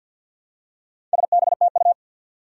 Morse code, heard as a keyed tone, about 50 wpm, 700 Hz. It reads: SBTU
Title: Four Challenging Letter-Number Combinations 50wpm